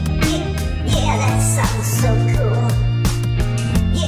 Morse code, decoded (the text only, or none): none